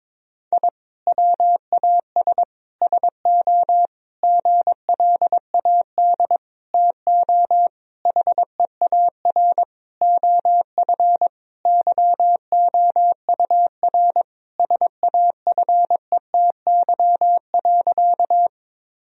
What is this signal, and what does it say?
Morse code, heard as a keyed tone, about 22 wpm, 700 Hz. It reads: I WAS SO GLAD TO HEAR OF YOUR SAFETY.
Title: Sentences from Top 1000 Words - Review Part 4 - 22wpm